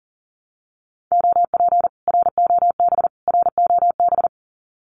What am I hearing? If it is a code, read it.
Morse code, heard as a keyed tone, about 40 wpm, 700 Hz. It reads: OP ROB ROB
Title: Common QSO Phrases Encore 40wpm